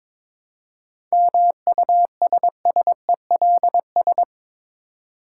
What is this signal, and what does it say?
Morse code, heard as a keyed tone, about 22 wpm, 700 Hz. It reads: MUSSELS
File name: Difficult Words with Hints - 22wpm